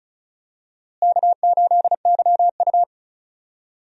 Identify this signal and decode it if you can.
Morse code, heard as a keyed tone, about 36 wpm, 700 Hz. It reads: K8YU
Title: USA Call Signs - 4 Characters - Repeated 3x - 35wpm